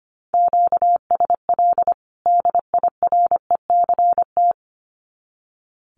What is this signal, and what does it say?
Morse code, heard as a keyed tone, about 25 wpm, 700 Hz. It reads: QSL DIRECT
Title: Common QSO Phrases 25wpm